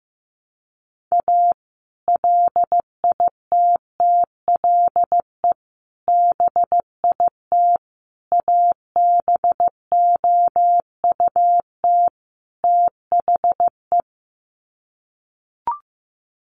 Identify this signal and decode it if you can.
Morse code, heard as a keyed tone, about 15 wpm, 700 Hz. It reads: A LITTLE BIT ABOUT THE
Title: Sets of 5 Words Encore2 15wpm